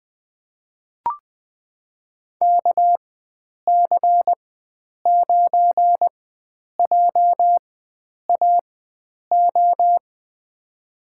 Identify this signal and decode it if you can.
Morse code, heard as a keyed tone, about 20 wpm, spaced as for 9 wpm, 700 Hz. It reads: KC9JAO
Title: USA Call Signs - 6 Characters - 4x Character Spacing - 20wpm